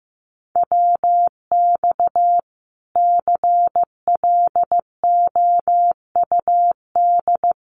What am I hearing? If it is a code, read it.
Morse code, heard as a keyed tone, about 15 wpm, 700 Hz. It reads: WX CLOUD
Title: Common QSO Phrases Encore 15wpm